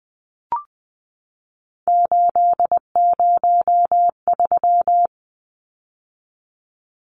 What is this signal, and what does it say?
Morse code, heard as a keyed tone, about 20 wpm, 700 Hz. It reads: 803
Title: Numbers - 3-Digit 20wpm